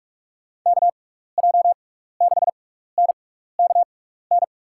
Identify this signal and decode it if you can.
Morse code, heard as a keyed tone, about 46 wpm, spaced as for 15 wpm, 700 Hz. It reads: KJ6NKN
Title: USA Call Signs - 6 Characters - 6x Character Spacing - 45wpm